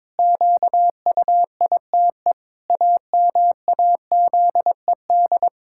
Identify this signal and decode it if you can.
Morse code, heard as a keyed tone, about 22 wpm, 700 Hz. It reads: QUITE AMAZED
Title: Pangram Challenge v2 - 22wpm